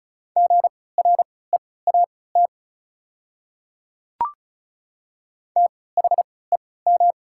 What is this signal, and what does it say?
Morse code, heard as a keyed tone, about 36 wpm, spaced as for 20 wpm, 690 Hz. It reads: GREAT THEM
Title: 3x Character Spacing - Top 200 Words - 35wpm